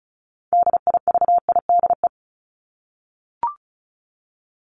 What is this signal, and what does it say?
Morse code, heard as a keyed tone, about 34 wpm, 700 Hz. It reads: DIVIDE
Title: Top 1501-1750 Words 35wpm